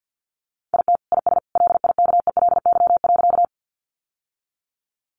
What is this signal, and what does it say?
Morse code, heard as a keyed tone, about 50 wpm, 700 Hz. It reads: IT IS LIKELY.
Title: Sentences from Top 600 Words 50wpm